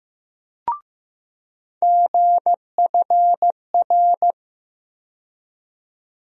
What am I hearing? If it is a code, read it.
Morse code, heard as a keyed tone, about 15 wpm, 700 Hz. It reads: GFR
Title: USA Call Sign Postfixes - 3 Letters 15wpm